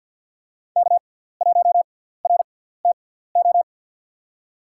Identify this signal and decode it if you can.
Morse code, heard as a keyed tone, about 47 wpm, spaced as for 17 wpm, 700 Hz. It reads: K1RTO
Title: USA Call Signs - 6x Character Spacing - 50wpm